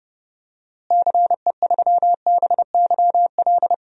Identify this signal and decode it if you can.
Morse code, heard as a keyed tone, about 32 wpm, 700 Hz. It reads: CE3BYL
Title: DXCC-100 Call Signs - 6 Characters - Repeated 3x - 30wpm